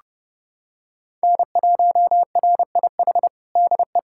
Morse code, heard as a keyed tone, about 32 wpm, 700 Hz. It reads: N1RIH DE
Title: USA Callsign DE Callsign 30wpm